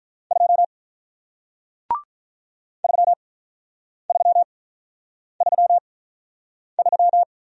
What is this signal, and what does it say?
Morse code, heard as a keyed tone, about 48 wpm, spaced as for 9 wpm, 700 Hz. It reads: 2 3333